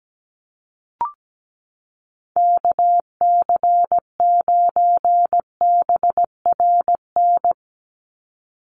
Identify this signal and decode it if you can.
Morse code, heard as a keyed tone, about 17 wpm, 700 Hz. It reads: KC9BRN
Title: USA Call Signs 17wpm